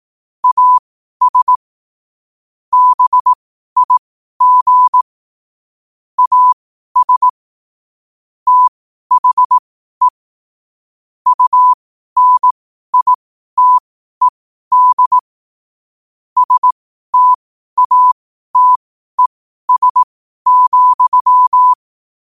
Morse code, heard as a keyed tone, about 18 wpm, spaced as for 12 wpm, 1000 Hz. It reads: AS BIG AS THE UNITED STATES,
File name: qotdfiles / 12wpm